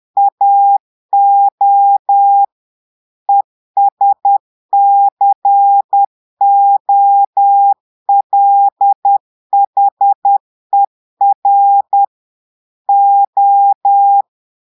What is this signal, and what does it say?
Morse code, heard as a keyed tone, about 10 wpm, 800 Hz. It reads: AO ESCOLHER O
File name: portugais texte en clair 2 vitesse 10 mots minutes